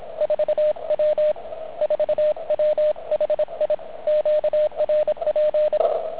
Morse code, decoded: 4W 4W HI QRP